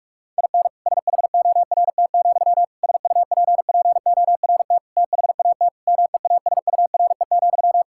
Morse code, heard as a keyed tone, about 46 wpm, 700 Hz. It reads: IN SHORT, SUPPORT THAT MEASURE,
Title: Sentences from Top 1000 Words - Review Part 5 - 45wpm